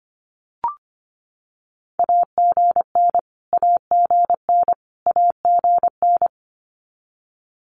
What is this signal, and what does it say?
Morse code, heard as a keyed tone, about 25 wpm, 700 Hz. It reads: AGN AGN AGN
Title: Abbreviations Repeated 3x - 25wpm